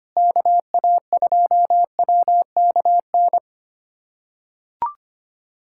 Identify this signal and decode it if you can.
Morse code, heard as a keyed tone, about 25 wpm, 700 Hz. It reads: KA2WKN
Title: USA Callsign DE Callsign 25wpm